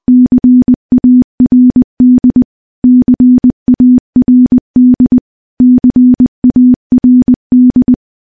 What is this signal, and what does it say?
Morse code, heard as a keyed tone, about 20 wpm, 250 Hz. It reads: CARD CARD CARD
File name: morse / morse-training-20WPM